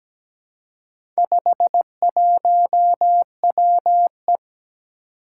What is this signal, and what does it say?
Morse code, heard as a keyed tone, about 17 wpm, 700 Hz. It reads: 51WE
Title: Four Challenging Letter-Number Combinations 17wpm